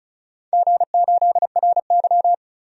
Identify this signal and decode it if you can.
Morse code, heard as a keyed tone, about 36 wpm, 700 Hz. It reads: G8RY
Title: DXCC-100 Call Signs - 4 Characters - 35wpm